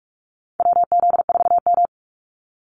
Morse code, heard as a keyed tone, about 44 wpm, 700 Hz. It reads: WZ4M
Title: NA SOTA Activator Callsigns 45wpm